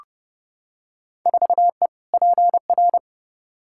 Morse code, heard as a keyed tone, about 29 wpm, 700 Hz. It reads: 4E PR
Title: Field Day Exchanges Encore - 30wpm